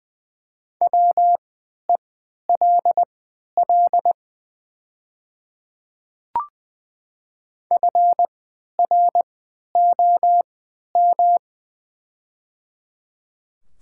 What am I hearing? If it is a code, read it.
Morse code, heard as a keyed tone, about 20 wpm, spaced as for 11 wpm, 700 Hz. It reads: WELL FROM
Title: 3x Character Spacing - Top 300 Words - 20wpm